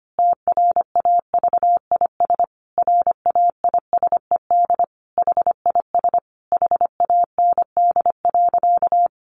Morse code, heard as a keyed tone, about 25 wpm, 700 Hz. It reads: TRAVIS RAISED HIS HAND.
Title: A Sound Of Thunder 25wpm